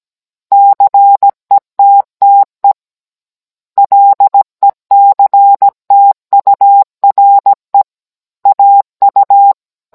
Morse code, heard as a keyed tone, about 17 wpm, 800 Hz. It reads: CETTE LECTURE AU